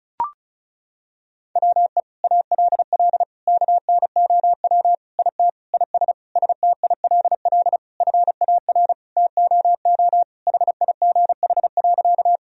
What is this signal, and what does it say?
Morse code, heard as a keyed tone, about 36 wpm, 700 Hz. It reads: WE ALL KNOW IT IS STILL FAR TOO HIGH.
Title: Sentences from Top 1000 Words - Review Part 1 - 35wpm